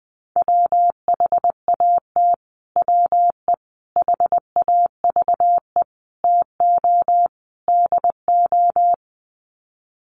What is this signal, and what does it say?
Morse code, heard as a keyed tone, about 20 wpm, 700 Hz. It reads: WHAT WE HAVE TO DO
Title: Sets of 5 Words Encore 20wpm